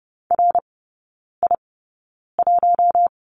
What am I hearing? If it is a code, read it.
Morse code, heard as a keyed tone, about 30 wpm, spaced as for 9 wpm, 700 Hz. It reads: RI1
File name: 081 Lesson 14e - Number 2 - Three Characters 30wpm